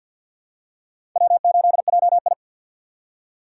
Morse code, heard as a keyed tone, about 48 wpm, 690 Hz. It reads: W8JI
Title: USA Call Signs - 2x Character Spacing - 50wpm